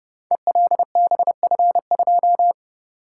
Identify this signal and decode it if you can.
Morse code, heard as a keyed tone, about 30 wpm, 700 Hz. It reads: ELBF2
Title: Code Groups - 5 Letter:Numbers - 30wpm